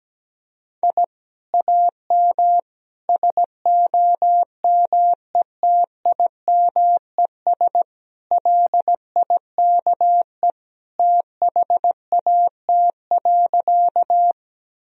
Morse code, heard as a keyed tone, about 17 wpm, 700 Hz. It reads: I AM SOMETIMES LIKE THAT.
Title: Sentences from Top 1000 Words - Review Part 3 - 17wpm